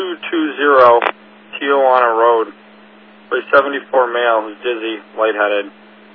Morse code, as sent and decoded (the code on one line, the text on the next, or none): none